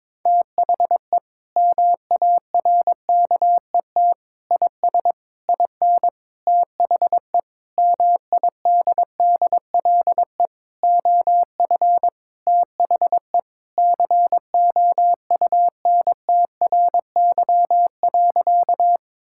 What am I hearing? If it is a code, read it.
Morse code, heard as a keyed tone, about 22 wpm, 700 Hz. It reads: THE MARKET IS IN THE MIDDLE OF THE COUNTRY.
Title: Sentences from Top 1000 Words - Review Part 6 - 22wpm